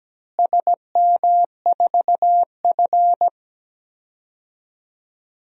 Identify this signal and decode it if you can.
Morse code, heard as a keyed tone, about 17 wpm, 700 Hz. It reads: SM4F
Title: DXCC-100 Call Signs - 4 Characters - Repeated 3x - 17wpm